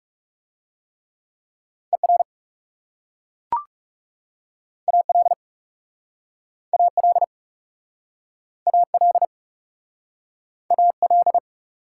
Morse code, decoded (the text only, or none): ER AL AL AL AL